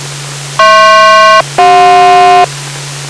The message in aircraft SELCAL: HP-CJ